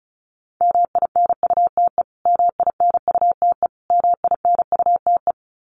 Morse code, heard as a keyed tone, about 34 wpm, 700 Hz. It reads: MINUTE MINUTE MINUTE
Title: Top 500 Words Repeated 35wpm